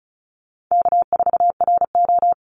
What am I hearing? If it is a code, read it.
Morse code, heard as a keyed tone, about 34 wpm, 700 Hz. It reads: K4RO
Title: Call Signs Repeated 3x - 35wpm